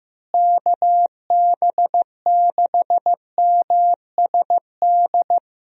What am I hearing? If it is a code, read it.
Morse code, heard as a keyed tone, about 15 wpm, 700 Hz. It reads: KB6MSD
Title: USA Call Signs - 6 Characters - Repeated 3x - 15wpm